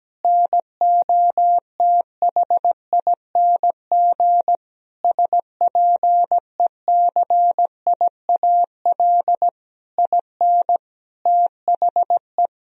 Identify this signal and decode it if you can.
Morse code, heard as a keyed tone, about 17 wpm, 700 Hz. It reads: NOTHING SPECIAL IN THE
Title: Sentences from Top 1000 Words - Review Part 3 - 17wpm